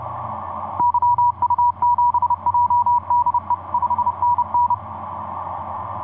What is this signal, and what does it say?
Morse code, heard as a keyed tone, about 30 wpm, 970 Hz. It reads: YU7JDE 5NN
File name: od5nj yu1cc etc 40m cw 14jan2010 1719